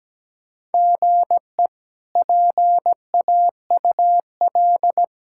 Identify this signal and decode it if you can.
Morse code, heard as a keyed tone, about 17 wpm, 700 Hz. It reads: GE PAUL
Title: K1USN SST QSO Practice - 17wpm